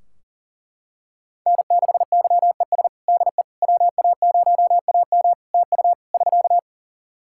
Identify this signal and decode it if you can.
Morse code, heard as a keyed tone, about 40 wpm, 700 Hz. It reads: N6YES DE WA0AM TU <SK>
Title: Practice QSO - Exchange 4 - Ending - 40wpm